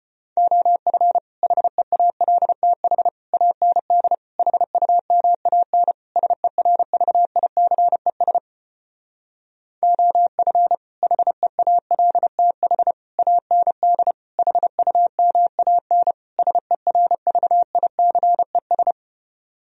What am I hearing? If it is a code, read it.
Morse code, heard as a keyed tone, about 33 wpm, 700 Hz. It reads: OF HEALTH AND HUMAN SERVICES OF HEALTH AND HUMAN SERVICES